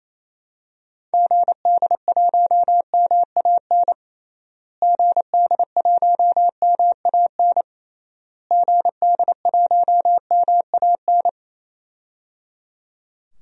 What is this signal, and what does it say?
Morse code, heard as a keyed tone, about 28 wpm, 700 Hz. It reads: GD1MAN GD1MAN GD1MAN
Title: DXCC-100 Call Signs - 6 Characters - Repeated 3x - 28wpm